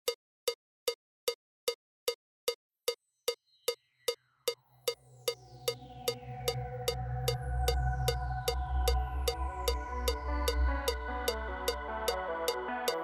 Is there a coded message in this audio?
No.